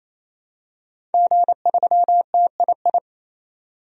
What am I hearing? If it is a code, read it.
Morse code, heard as a keyed tone, about 28 wpm, 700 Hz. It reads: G3TII
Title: DXCC-100 Call Signs - 5 Characters - 28wpm